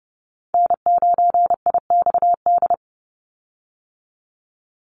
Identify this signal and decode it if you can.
Morse code, heard as a keyed tone, about 30 wpm, 700 Hz. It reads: N9IXD
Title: Call Signs Repeated 2x - 30wpm